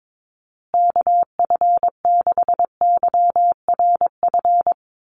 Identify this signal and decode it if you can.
Morse code, heard as a keyed tone, about 22 wpm, 700 Hz. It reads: KF6YRF
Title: USA Call Signs Repeated 2x 22wpm